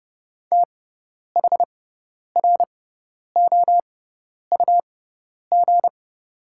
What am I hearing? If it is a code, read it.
Morse code, heard as a keyed tone, about 30 wpm, spaced as for 10 wpm, 700 Hz. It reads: THROUG
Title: 6x Character Spacing - Top 200 Words - 30wpm